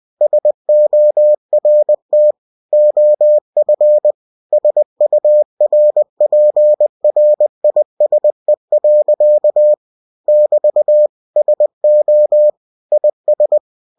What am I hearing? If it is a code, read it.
Morse code, heard as a keyed tone, about 20 wpm, 600 Hz. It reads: SORT OF SURPRISE. <BT> SO IS
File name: cw-books / winnie-the-pooh / winnie-20wpm-page-0082